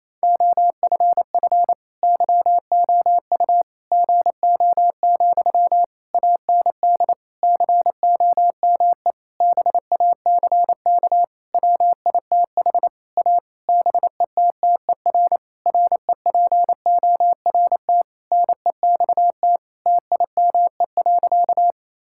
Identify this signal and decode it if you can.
Morse code, heard as a keyed tone, about 28 wpm, 700 Hz. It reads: OFF YOU GO, AND COME BACK WITH A BETTER REPORT NEXT TIME.
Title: Sentences from Top 1000 Words - Review Part 6 - 28wpm